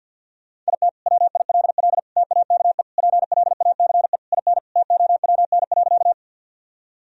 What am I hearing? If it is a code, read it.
Morse code, heard as a keyed tone, about 52 wpm, 700 Hz. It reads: IT WILL TAKE PLACE IN TOWN.
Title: Sentences from Top 1000 Words - Review Part 2 - 50wpm